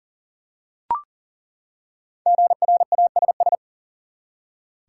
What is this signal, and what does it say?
Morse code, heard as a keyed tone, about 38 wpm, 700 Hz. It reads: GRASS
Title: Five Letter Words - 40wpm